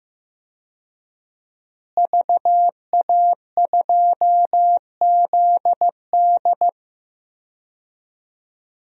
Morse code, heard as a keyed tone, about 15 wpm, 700 Hz. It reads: VA2ZD
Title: DXCC-100 Call Signs - 5 Characters - Repeated 3x - 15wpm